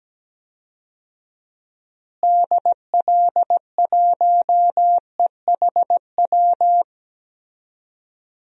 Morse code, heard as a keyed tone, about 17 wpm, 700 Hz. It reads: DL1EHW
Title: DXCC-100 Call Signs - 6 Characters - Repeated 2x - 17wpm